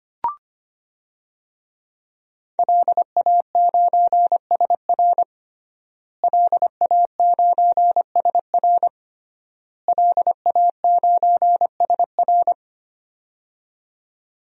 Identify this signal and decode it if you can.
Morse code, heard as a keyed tone, about 25 wpm, 700 Hz. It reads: LA9SR LA9SR LA9SR
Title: DXCC-100 Call Signs - 5 Characters - Repeated 3x - 25wpm